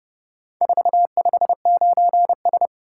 Morse code, heard as a keyed tone, about 28 wpm, 700 Hz. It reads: 459S